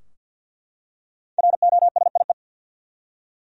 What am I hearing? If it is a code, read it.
Morse code, heard as a keyed tone, about 48 wpm, 700 Hz. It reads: ROSIE
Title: USA Common Names - 50wpm